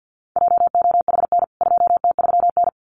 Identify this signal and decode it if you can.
Morse code, heard as a keyed tone, about 48 wpm, 700 Hz. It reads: JOHN 2T3N
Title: CWT Exchange 50wpm